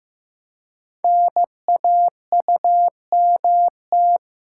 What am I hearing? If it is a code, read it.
Morse code, heard as a keyed tone, about 15 wpm, 700 Hz. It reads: NAUMT
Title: Code Groups - 5 Letters - 15wpm